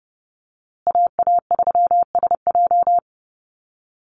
Morse code, heard as a keyed tone, about 30 wpm, 700 Hz. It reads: AA3SJ
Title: USA Call Signs Repeated 3x 30wpm